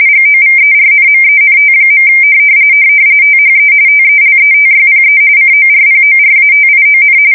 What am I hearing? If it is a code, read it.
RTTY, about 45.45 baud. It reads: GOVERNMENT. THE TELETYPE CORPORATION
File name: fsk170 / teletype-corp